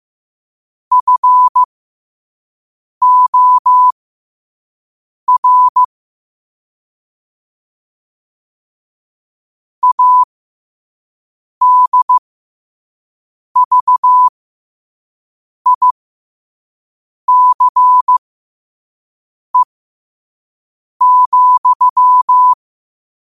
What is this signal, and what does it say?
Morse code, heard as a keyed tone, about 15 wpm, spaced as for 5 wpm, 1000 Hz. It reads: FOR ADVICE,
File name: qotdfiles / 5wpm